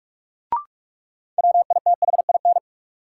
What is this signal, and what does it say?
Morse code, heard as a keyed tone, about 44 wpm, 700 Hz. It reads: WITHIN
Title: Sentences from Top 700 Words - New Word Drill - 45wpm